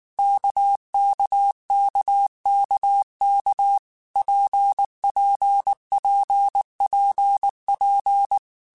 Morse code, heard as a keyed tone, about 19 wpm, 800 Hz. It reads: KKKKK PPPPP
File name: Basic Letter Groups By7Groups / RKPX / RKPX@18